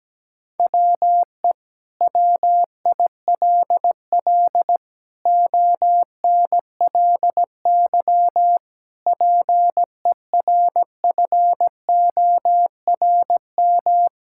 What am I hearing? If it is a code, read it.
Morse code, heard as a keyed tone, about 17 wpm, 700 Hz. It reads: WE WILL ONLY PERFORM
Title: Sentences from Top 1000 Words - Review Part 5 - 17wpm